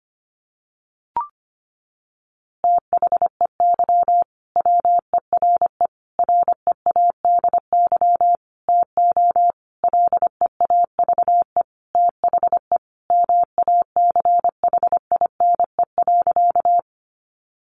Morse code, heard as a keyed tone, about 25 wpm, 700 Hz. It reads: THEY WERE READY TO LEAVE THE MACHINE.
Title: A Sound Of Thunder 25wpm